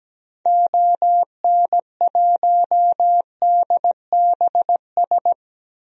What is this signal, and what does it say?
Morse code, heard as a keyed tone, about 17 wpm, 700 Hz. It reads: ON1DBS
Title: DXCC-100 Call Signs - 6 Characters - Repeated 2x - 17wpm